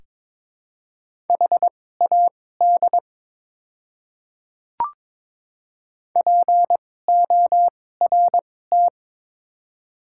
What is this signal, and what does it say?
Morse code, heard as a keyed tone, about 22 wpm, spaced as for 16 wpm, 700 Hz. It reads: HAD PORT
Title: 2x Character Spacing - Top 200 Words - 22wpm